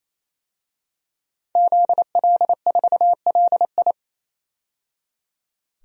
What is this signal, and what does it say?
Morse code, heard as a keyed tone, about 28 wpm, 700 Hz. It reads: ZL4LI
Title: DXCC-100 Call Signs - 5 Characters - 28wpm